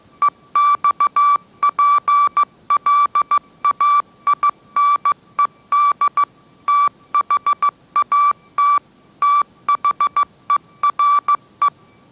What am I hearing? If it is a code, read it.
Morse code, heard as a keyed tone, about 15 wpm, 1160 Hz. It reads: EXPLAINED THAT THERE
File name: mrx / mrx066 16wpm